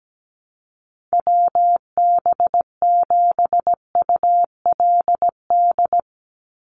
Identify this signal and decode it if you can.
Morse code, heard as a keyed tone, about 17 wpm, 700 Hz. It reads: WB7ULD
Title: NA SOTA Chaser Callsigns 17wpm